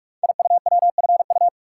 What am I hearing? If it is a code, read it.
Morse code, heard as a keyed tone, about 44 wpm, 700 Hz. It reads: IUWFU
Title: Code Groups - 5 Letters - 45wpm